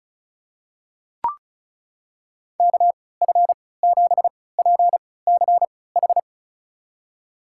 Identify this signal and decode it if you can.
Morse code, heard as a keyed tone, about 36 wpm, spaced as for 20 wpm, 700 Hz. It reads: KF7PCH
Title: USA Call Signs - 6 Characters - 3x Character Spacing - 35wpm